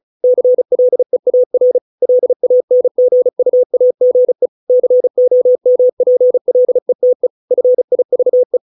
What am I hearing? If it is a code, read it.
Morse code, heard as a keyed tone, about 35 wpm, 500 Hz. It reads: CLEAR LANGUAGE COMPLETE FIVE